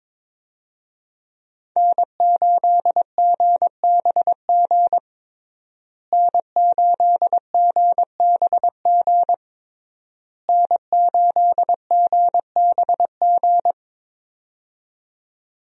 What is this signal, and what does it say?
Morse code, heard as a keyed tone, about 22 wpm, 700 Hz. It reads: N8GBG N8GBG N8GBG
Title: USA Call Signs - 5 Characters - Repeated 3x - 22wpm